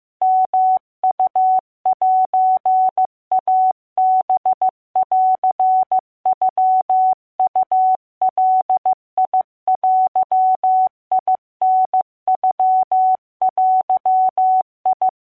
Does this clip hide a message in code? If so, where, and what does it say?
Yes: Morse code (Japanese Wabun code) from 0.2 to 15.1 s, heard as a keyed tone, about 15 wpm, 750 Hz. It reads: ヨウセイハンノウガデタノデ